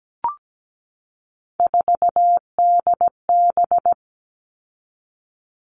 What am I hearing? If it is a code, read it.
Morse code, heard as a keyed tone, about 17 wpm, 700 Hz. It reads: 4DB
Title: All Cut Numbers - 3-Digit 17wpm